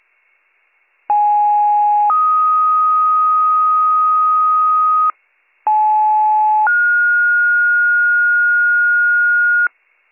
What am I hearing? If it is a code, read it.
Two-tone page, heard as a one-second tone, then a three-second tone, about 830 then 1250 Hz, about 860 then 1430 Hz, with noise. It is heard from 1.1 to 9.7 s.